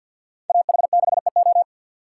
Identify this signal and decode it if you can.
Morse code, heard as a keyed tone, about 61 wpm, 700 Hz. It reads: AH6EO
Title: Speed Racing - USA Call Signs 50wpm